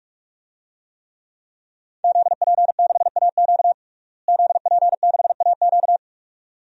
Morse code, heard as a keyed tone, about 41 wpm, 700 Hz. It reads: ZP6AQ ZP6AQ
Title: DXCC-100 Call Signs - 5 Characters - Repeated 2x - 45wpm